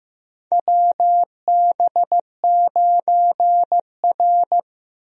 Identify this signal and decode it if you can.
Morse code, heard as a keyed tone, about 15 wpm, 700 Hz. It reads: WB9R
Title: USA Call Signs - 4 Characters - 15wpm